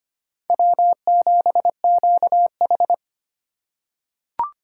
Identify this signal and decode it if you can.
Morse code, heard as a keyed tone, about 25 wpm, 700 Hz. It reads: W7QH
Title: USA Call Signs - 4 Characters - 2x Character Spacing - 25wpm